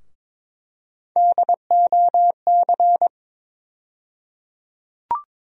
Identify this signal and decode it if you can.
Morse code, heard as a keyed tone, about 22 wpm, 700 Hz. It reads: DOC
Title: USA Call Sign Postfixes - 3 Letters 22wpm